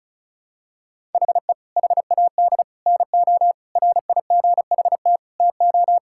Morse code, heard as a keyed tone, about 34 wpm, 700 Hz. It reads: HE HAD NO RIGHT TO
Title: Sentences from Top 1000 Words - Review Part 5 - 35wpm